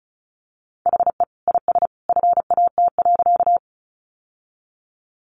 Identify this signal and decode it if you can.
Morse code, heard as a keyed tone, about 34 wpm, 700 Hz. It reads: HE IS FAT.
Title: Sentences from Top 600 Words 35wpm